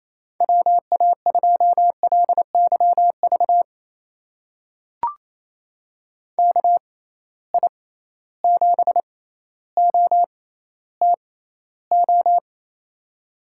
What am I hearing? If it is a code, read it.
Morse code, heard as a keyed tone, about 28 wpm, spaced as for 14 wpm, 700 Hz. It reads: WA2LYV KI7OTO
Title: USA Call Signs - 6 Characters - 6x Character Spacing - 28wpm